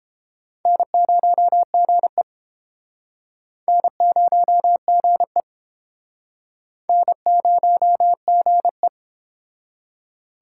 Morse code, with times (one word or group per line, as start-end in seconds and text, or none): 0.7-2.2 N0GE
3.7-5.4 N0GE
6.9-8.9 N0GE